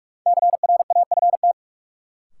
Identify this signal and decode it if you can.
Morse code, heard as a keyed tone, about 46 wpm, 700 Hz. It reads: CRAFT
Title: Sentences from Top 1000 Words - Review Part 6 - New Word Drill - 45wpm